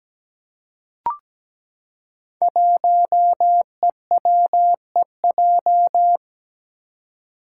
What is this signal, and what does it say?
Morse code, heard as a keyed tone, about 17 wpm, 700 Hz. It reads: 1EWEJ
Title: Five Challenging Letter-Number Combinations 17wpm